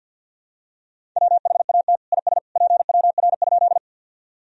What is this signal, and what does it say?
Morse code, heard as a keyed tone, about 49 wpm, 700 Hz. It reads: WHAT IS PWR?
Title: Sentences from Top 500 Words and Abbreviations 50wpm